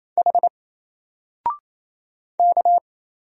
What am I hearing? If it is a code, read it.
Morse code, heard as a keyed tone, about 27 wpm, 700 Hz. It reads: H K
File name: Single-Letter-Number-28wpm